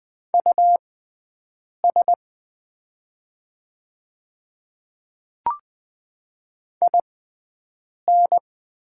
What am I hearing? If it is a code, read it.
Morse code, heard as a keyed tone, about 20 wpm, spaced as for 7 wpm, 700 Hz. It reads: US IN
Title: 6x Character Spacing - Top 200 Words - 20wpm